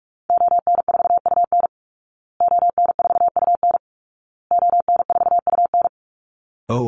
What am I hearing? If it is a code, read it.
Morse code, heard as a keyed tone, about 44 wpm, 700 Hz. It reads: ON4UN ON4UN ON4UN
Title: Call Signs Repeated 2x - 45wpm